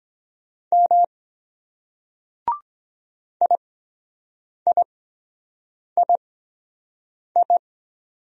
Morse code, heard as a keyed tone, about 23 wpm, spaced as for 7 wpm, 700 Hz. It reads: M IIII